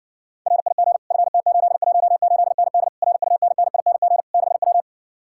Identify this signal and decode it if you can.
Morse code, heard as a keyed tone, about 70 wpm, 700 Hz. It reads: RIG FT818ND RUNNING 6W